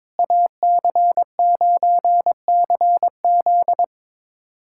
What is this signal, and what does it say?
Morse code, heard as a keyed tone, about 21 wpm, 700 Hz. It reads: AC9CZ
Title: USA Callsign DE Callsign - 5 Characters - 22wpm